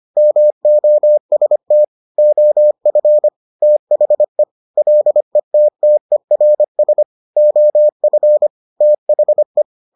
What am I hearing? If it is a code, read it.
Morse code, heard as a keyed tone, about 25 wpm, 600 Hz. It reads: MOST OF THE LETTERS OF THE